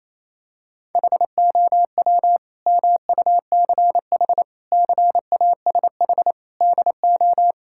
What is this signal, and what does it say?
Morse code, heard as a keyed tone, about 28 wpm, 700 Hz. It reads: HOW MUCH CASH DO
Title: Sentences from Top 1000 Words - Review Part 5 - 28wpm